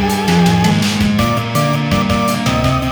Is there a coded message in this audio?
No.